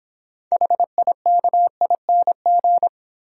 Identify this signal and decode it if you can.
Morse code, heard as a keyed tone, about 25 wpm, 700 Hz. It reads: HIKING
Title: Easy Words - Speed-Racing with Hints - 20wpm